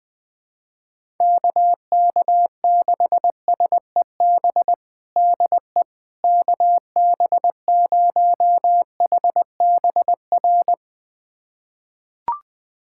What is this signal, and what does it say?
Morse code, heard as a keyed tone, about 20 wpm, 700 Hz. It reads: KK6SEB DE KB0HBR
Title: USA Callsign DE Callsign - 6 Characters - 20wpm